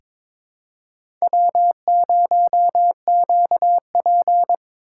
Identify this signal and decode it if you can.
Morse code, heard as a keyed tone, about 22 wpm, 700 Hz. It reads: W0QP
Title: USA Call Signs - 4 Characters - 8x Character Spacing - 22wpm